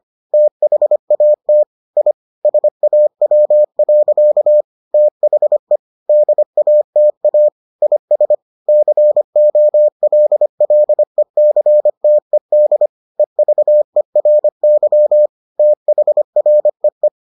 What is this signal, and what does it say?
Morse code, heard as a keyed tone, about 25 wpm, 600 Hz. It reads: THAT I SAW. THE DATA IS COLLECTED EVERY THREE